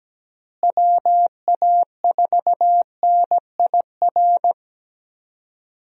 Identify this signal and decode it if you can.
Morse code, heard as a keyed tone, about 17 wpm, 700 Hz. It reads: WA4NIR
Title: USA Call Signs - 6 Characters - Repeated 3x - 17wpm